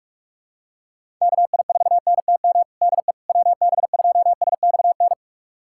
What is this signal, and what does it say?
Morse code, heard as a keyed tone, about 42 wpm, 700 Hz. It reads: KI4NTM DE WB2SXN
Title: USA Callsign DE Callsign 45wpm